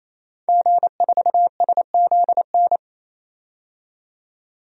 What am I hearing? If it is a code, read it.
Morse code, heard as a keyed tone, about 27 wpm, 700 Hz. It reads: G4SZN
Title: DXCC-100 Call Signs - 5 Characters - 28wpm